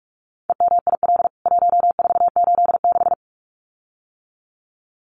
Morse code, heard as a keyed tone, about 44 wpm, 700 Hz. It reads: EMIL 1486
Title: CWT Exchange 45wpm